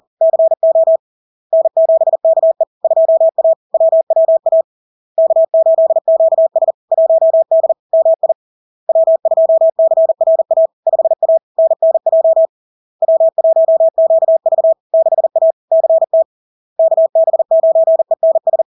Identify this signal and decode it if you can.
Morse code, heard as a keyed tone, about 40 wpm, 650 Hz. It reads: CO N7KE 2A WWA K8QS 1D MI W2CRA 5A NNJ W1QV 6A CT KB9ENS
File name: LARK-FD-2022-Log-in-CW-40wpm-650-Hz